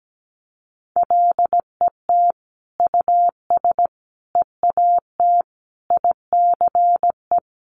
Morse code, heard as a keyed tone, about 17 wpm, 700 Hz. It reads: LET US EAT ICE
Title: Sentences from Top 1500 Words 17wpm